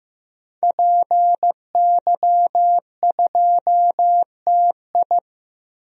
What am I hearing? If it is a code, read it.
Morse code, heard as a keyed tone, about 15 wpm, 700 Hz. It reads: PY2TI
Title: DXCC-100 Call Signs - 5 Characters - 15wpm